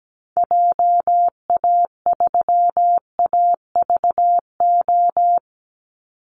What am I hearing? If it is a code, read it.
Morse code, heard as a keyed tone, about 17 wpm, 700 Hz. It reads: JA3AVO
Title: Call Signs - 17wpm